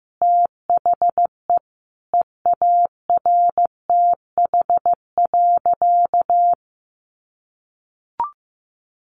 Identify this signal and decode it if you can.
Morse code, heard as a keyed tone, about 15 wpm, 700 Hz. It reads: THE EARTH.
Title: Sentences from Top 1200 Words 15wpm